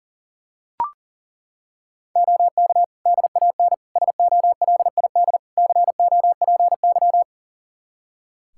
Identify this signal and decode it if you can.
Morse code, heard as a keyed tone, about 40 wpm, 700 Hz. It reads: OK DAN SOLID COPY
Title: Practice QSO - Exchange 3 - More Info - 40wpm